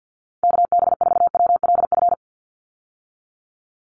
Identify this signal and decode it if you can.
Morse code, heard as a keyed tone, about 49 wpm, 700 Hz. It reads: KB4WLF
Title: Call Signs - 50wpm